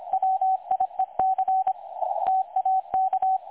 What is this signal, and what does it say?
Morse code, heard as a keyed tone, about 25 wpm, 760 Hz, with radio noise. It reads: WIEC TAK
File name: dlazaawansowanych / nagraniazpasma / websdr recording 3565.0kHz